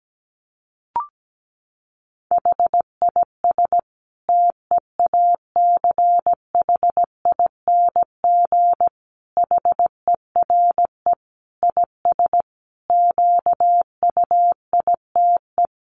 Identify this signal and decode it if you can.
Morse code, heard as a keyed tone, about 17 wpm, 700 Hz. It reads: HIS TEACHING HERE IS QUITE
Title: Sentences from Top 1100 Words 17wpm